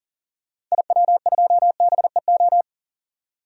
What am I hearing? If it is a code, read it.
Morse code, heard as a keyed tone, about 40 wpm, 700 Hz. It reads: IW2BEO
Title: DXCC-100 Call Signs - 6 Characters - 40wpm